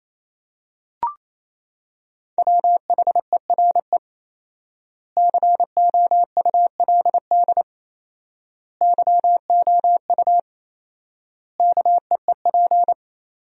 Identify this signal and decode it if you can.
Morse code, heard as a keyed tone, about 28 wpm, 700 Hz. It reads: WHERE COULD YOU KEEP
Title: 4x Word-Spacing - Sentences from Top 300 Words - 28wpm